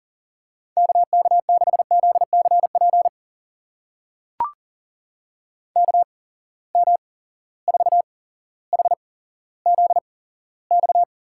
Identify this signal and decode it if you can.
Morse code, heard as a keyed tone, about 39 wpm, spaced as for 17 wpm, 700 Hz. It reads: KK6ZCP KM4HZX
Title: USA Call Signs - 6 Characters - 8x Character Spacing - 40wpm